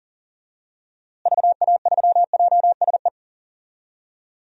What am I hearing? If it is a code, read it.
Morse code, heard as a keyed tone, about 37 wpm, 700 Hz. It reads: VA3JSE
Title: DXCC-100 Call Signs - 6 Characters - Repeated 3x - 40wpm